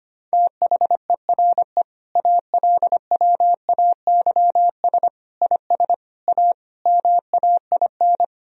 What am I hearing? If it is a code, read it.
Morse code, heard as a keyed tone, about 25 wpm, 700 Hz. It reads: THERE ALWAYS IS A MAIN
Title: Sentences from Top 1000 Words - Review Part 2 - 25wpm